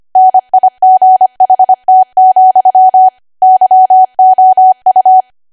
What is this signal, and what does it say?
Morse code, heard as a keyed tone, about 25 wpm, 750 Hz. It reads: NIGHT, YOU
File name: Archive / 25 WPM / 25WPM